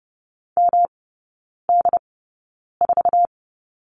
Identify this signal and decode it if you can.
Morse code, heard as a keyed tone, about 30 wpm, spaced as for 9 wpm, 700 Hz. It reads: MD4
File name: Lesson 18e - Letter m - Three Characters 30wpm